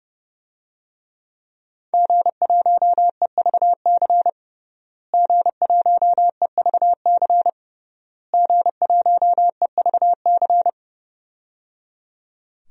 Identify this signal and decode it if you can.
Morse code, heard as a keyed tone, about 30 wpm, 700 Hz. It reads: G1EVC G1EVC G1EVC
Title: DXCC-100 Call Signs - 5 Characters - Repeated 3x - 30wpm